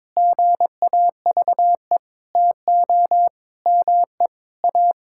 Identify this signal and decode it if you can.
Morse code, heard as a keyed tone, about 22 wpm, 700 Hz. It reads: GAVE TO ME A
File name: Sentences with 9-Word Phrases - 22wpm